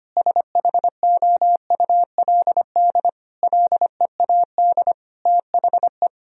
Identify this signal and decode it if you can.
Morse code, heard as a keyed tone, about 25 wpm, 700 Hz. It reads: SHOULD LEAD THE
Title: Sentences from Top 1000 Words - Review Part 2 - 25wpm